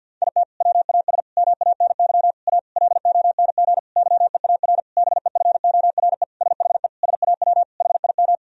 Morse code, heard as a keyed tone, about 49 wpm, 700 Hz. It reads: IT WAS MANY A LONG YEAR BEFORE SHE SAW HIM
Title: Sentences from Top 1000 Words - Review Part 1 - 50wpm